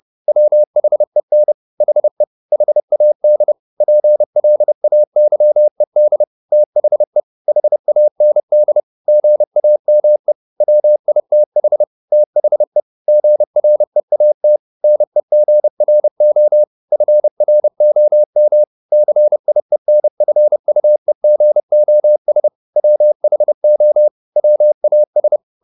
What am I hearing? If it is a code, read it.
Morse code, heard as a keyed tone, about 30 wpm, 600 Hz. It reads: WHEN HE HAD PLAYED THE HAND GAME WITH THE GREAT NEGRO FROM CIENFUEGOS WHO WAS
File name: the old man and the sea / the old man and the sea 1 30wpm